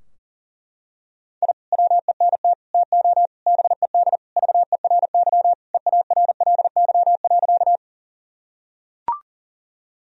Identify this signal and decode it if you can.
Morse code, heard as a keyed tone, about 40 wpm, 700 Hz. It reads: I WENT TO BED VERY EARLY.
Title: Sentences from Top 1000 Words - Review Part 2 - 40wpm